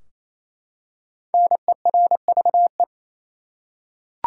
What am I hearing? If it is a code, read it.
Morse code, heard as a keyed tone, about 28 wpm, 700 Hz. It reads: NERVE
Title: Five Letter Words - 28wpm